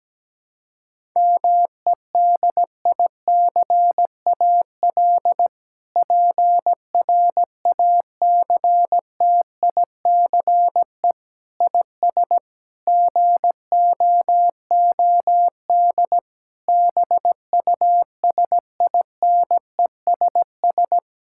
Morse code, heard as a keyed tone, about 17 wpm, 700 Hz. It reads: MEDICAL PRACTICE IS GOOD BUSINESS
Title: Sentences from Top 1000 Words - Review Part 5 - 17wpm